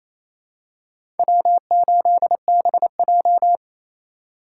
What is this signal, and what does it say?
Morse code, heard as a keyed tone, about 27 wpm, 700 Hz. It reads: W8BJ
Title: USA Call Signs - 4 Characters - 4x Character Spacing - 28wpm